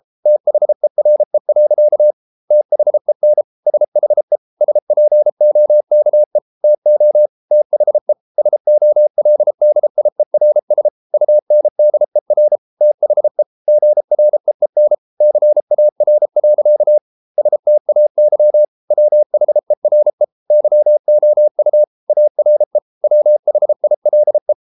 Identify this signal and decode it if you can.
Morse code, heard as a keyed tone, about 33 wpm, 600 Hz. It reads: THERE. THEN SHE SPOKE TO THE SOLDIERS UNDER THE GREEN CAR. STAY WHERE YOU ARE WHILE